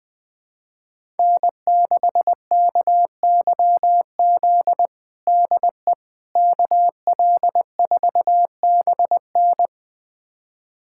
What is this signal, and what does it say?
Morse code, heard as a keyed tone, about 20 wpm, 700 Hz. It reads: N6KYZ DE KL4BN
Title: USA Callsign DE Callsign - 5 Characters - 20wpm